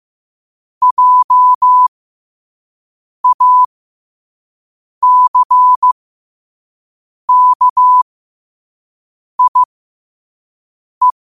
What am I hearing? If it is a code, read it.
Morse code, heard as a keyed tone, about 15 wpm, spaced as for 5 wpm, 1000 Hz. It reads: JACKIE
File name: qotdfiles / 5wpm